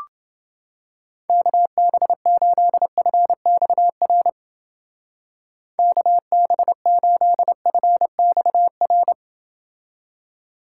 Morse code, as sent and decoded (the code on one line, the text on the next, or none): -.- -... ---.. ..-. -..- .-. / -.- -... ---.. ..-. -..- .-.
KB8FXR KB8FXR